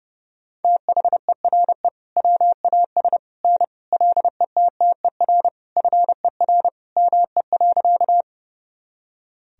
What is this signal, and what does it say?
Morse code, heard as a keyed tone, about 30 wpm, 700 Hz. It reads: THERE WAS N LETTER FER ME.
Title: Sentences from Top 300 Words and Abbreviations 30wpm